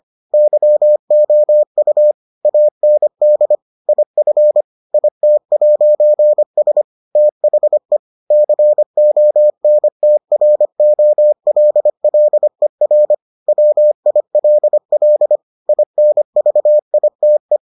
Morse code, heard as a keyed tone, about 25 wpm, 600 Hz. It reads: YOU AND IF IT'S THE CONTROLLER WILL INVITE